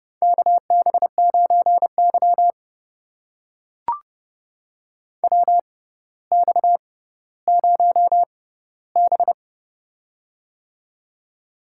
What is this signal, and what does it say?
Morse code, heard as a keyed tone, about 30 wpm, spaced as for 15 wpm, 700 Hz. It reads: KB9Y WX0B
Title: USA Call Signs - 6x Character Spacing - 30wpm